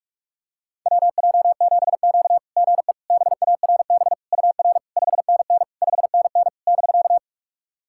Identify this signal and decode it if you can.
Morse code, heard as a keyed tone, about 42 wpm, 690 Hz. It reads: WJ7Q GE BARB UR 5NN 5NN <BK>
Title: SOTA QSO Practice 45wpm